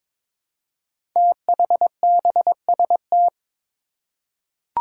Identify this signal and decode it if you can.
Morse code, heard as a keyed tone, about 22 wpm, 700 Hz. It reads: THBST
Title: Code Groups - 5 Letters - 22wpm